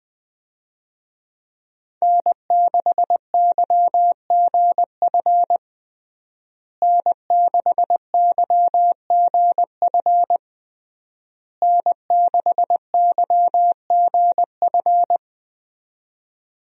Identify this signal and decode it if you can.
Morse code, heard as a keyed tone, about 21 wpm, 700 Hz. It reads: N6YGF N6YGF N6YGF
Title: USA Call Signs - 5 Characters - Repeated 3x - 20wpm